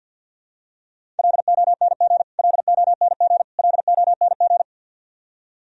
Morse code, heard as a keyed tone, about 51 wpm, 700 Hz. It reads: LONG LONG LONG
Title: Top 100 Words Repeated 50wpm